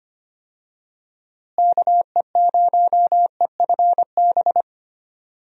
Morse code, heard as a keyed tone, about 25 wpm, 700 Hz. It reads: KE0EFB
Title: USA Call Signs - 6 Characters - Repeated 3x - 25wpm